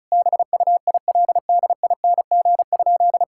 Morse code, heard as a keyed tone, about 34 wpm, 700 Hz. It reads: BUILDING?
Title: Sentences from Top 1000 Words - Review Part 5 - 35wpm